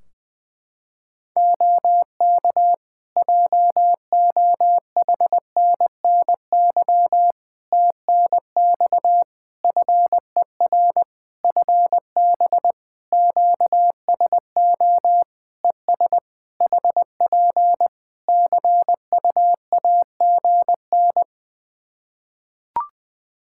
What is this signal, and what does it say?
Morse code, heard as a keyed tone, about 20 wpm, 700 Hz. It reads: OK JOHNNY TNX FER FB QSO ES HP CUAGN
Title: Practice QSO - Exchange 4 - Ending - 20wpm